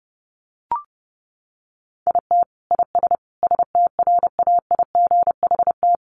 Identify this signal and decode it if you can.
Morse code, heard as a keyed tone, about 28 wpm, 700 Hz. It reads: IT IS STRAIGHT
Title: Sentences from Top 1000 Words 30wpm